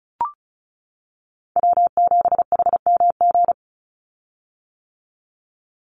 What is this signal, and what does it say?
Morse code, heard as a keyed tone, about 36 wpm, 700 Hz. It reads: W7HMG
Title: USA Call Signs 35wpm